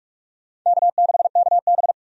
Morse code, heard as a keyed tone, about 46 wpm, 700 Hz. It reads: K6KB
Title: USA Call Signs - 4 Characters - 45wpm